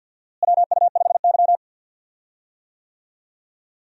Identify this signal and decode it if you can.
Morse code, heard as a keyed tone, about 51 wpm, 700 Hz. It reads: WU5Y
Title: USA Call Signs - 4 Characters - Repeated 3x - 50wpm